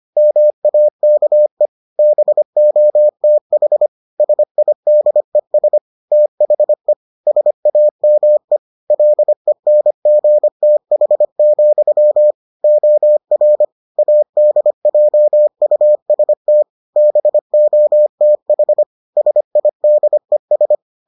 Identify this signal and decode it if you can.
Morse code, heard as a keyed tone, about 25 wpm, 600 Hz. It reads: MAKE BOTH SIDES THE SAME LENGTH, OR ADJUST BOTH SIDES